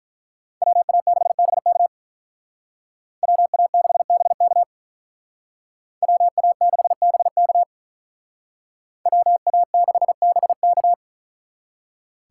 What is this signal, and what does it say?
Morse code, heard as a keyed tone, about 42 wpm, 700 Hz. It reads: WA6BK WA6BK WA6BK WA6BK
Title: Speed Racing - USA 2x2 Call Signs 35wpm